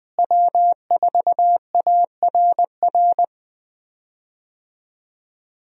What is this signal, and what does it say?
Morse code, heard as a keyed tone, about 20 wpm, 700 Hz. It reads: W4ARR
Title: USA Call Signs - 5 Characters - Repeated 3x - 20wpm